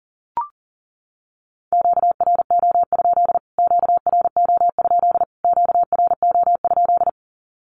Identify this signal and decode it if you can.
Morse code, heard as a keyed tone, about 40 wpm, 700 Hz. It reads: QRO? QRO? QRO?
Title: Q-Codes Repeated 3x 40wpm